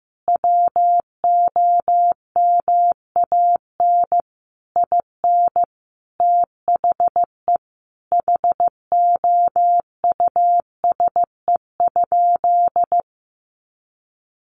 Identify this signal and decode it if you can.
Morse code, heard as a keyed tone, about 15 wpm, 700 Hz. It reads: WOMAN IN THE HOUSE?
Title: Sentences from Top 1100 Words 15wpm